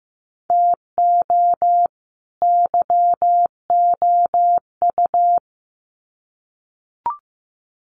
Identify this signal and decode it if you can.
Morse code, heard as a keyed tone, about 15 wpm, 700 Hz. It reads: TO YOU
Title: Sets of 2 Words 15wpm